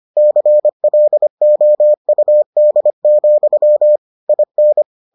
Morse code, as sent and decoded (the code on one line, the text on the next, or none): -.-. .-.. --- ..- -.. --..-- / .. -.
CLOUD, IN